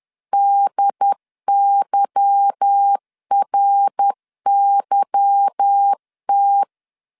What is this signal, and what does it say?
Morse code, heard as a keyed tone, about 10 wpm, 800 Hz. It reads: DYRYT
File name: LEZIONI / 8 YJ45 / 8 Y 10wpm 2615